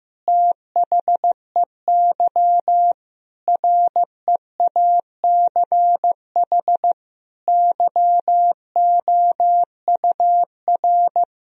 Morse code, heard as a keyed tone, about 15 wpm, 700 Hz. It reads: THEY REACH YOUR
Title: Sentences from Top 1000 Words - Review Part 4 - 15wpm